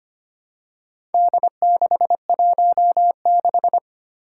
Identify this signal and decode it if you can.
Morse code, heard as a keyed tone, about 25 wpm, 700 Hz. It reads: D616
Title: Four Challenging Letter-Number Combinations 25wpm